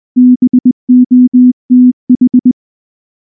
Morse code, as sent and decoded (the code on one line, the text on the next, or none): -... --- - ....
BOTH